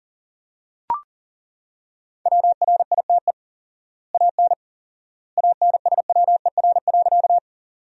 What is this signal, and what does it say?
Morse code, heard as a keyed tone, about 40 wpm, 700 Hz. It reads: WRITE AN ANSWER.
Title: 4x Word-Spacing - Sentences from Top 300 Words - 40wpm